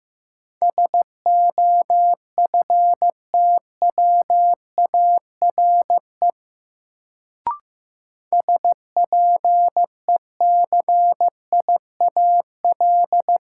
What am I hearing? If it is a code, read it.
Morse code, heard as a keyed tone, about 15 wpm, 700 Hz. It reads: SOFTWARE SPECIAL
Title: Sentences from Top 1000 Words - Review Part 3 - New Word Drill - 15wpm